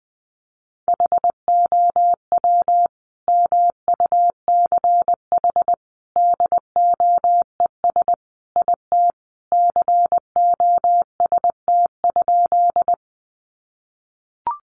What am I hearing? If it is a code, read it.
Morse code, heard as a keyed tone, about 20 wpm, 700 Hz. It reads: HOW MUCH DOES IT COST?
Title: Sentences from Top 800 Words 20wpm